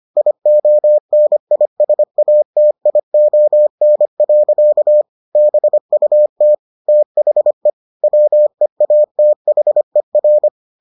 Morse code, heard as a keyed tone, about 25 wpm, 600 Hz. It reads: IONISATION. BUT THE WEATHER